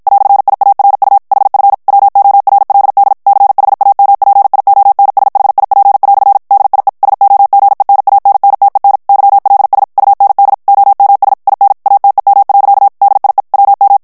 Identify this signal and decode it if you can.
Morse code, heard as a keyed tone, about 54 wpm, 800 Hz. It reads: "IARU HF WORLD CHAMPIONSHIP" DIE SOGENANNTEN YLS UND OMS IN ATEM. DIE WM
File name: morse mp3 / ALLEMAND texte clair 2 vitesse 50 mots minute